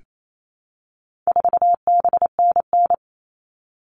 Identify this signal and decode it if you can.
Morse code, heard as a keyed tone, about 28 wpm, 700 Hz. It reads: 4BNN
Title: All Cut Numbers - 4-Digit 28wpm